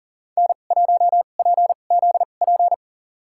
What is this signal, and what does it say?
Morse code, heard as a keyed tone, about 40 wpm, spaced as for 29 wpm, 700 Hz. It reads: N1PZP
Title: USA Call Signs - 5 Characters - 2x Character Spacing - 40wpm